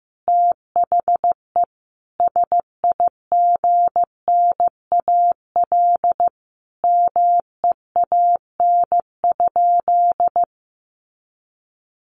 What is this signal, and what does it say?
Morse code, heard as a keyed tone, about 15 wpm, 700 Hz. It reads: THE SIGNAL MEAN?
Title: Sentences from Top 1400 Words 15wpm